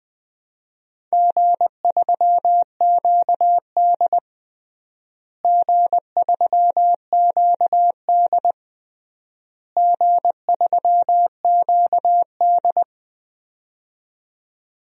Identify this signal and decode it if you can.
Morse code, heard as a keyed tone, about 20 wpm, 700 Hz. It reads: G3QD G3QD G3QD
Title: DXCC-100 Call Signs - 4 Characters - Repeated 3x - 20wpm